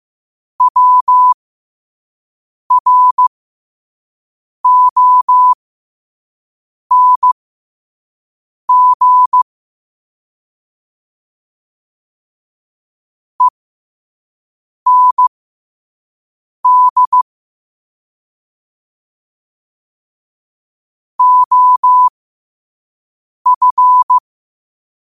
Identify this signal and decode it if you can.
Morse code, heard as a keyed tone, about 15 wpm, spaced as for 5 wpm, 1000 Hz. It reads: WRONG END OF
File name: qotdfiles / 5wpm